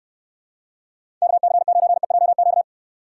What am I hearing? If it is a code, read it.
Morse code, heard as a keyed tone, about 69 wpm, 700 Hz. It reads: KC9EJQ